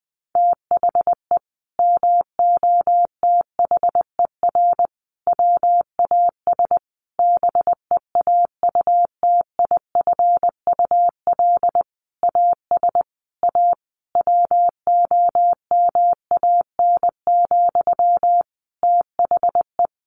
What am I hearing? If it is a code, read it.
Morse code, heard as a keyed tone, about 20 wpm, 700 Hz. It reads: THE MOTHER WAS BEAUTIFUL AS A WOMAN, THE
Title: Sentences from Top 1500 Words 20wpm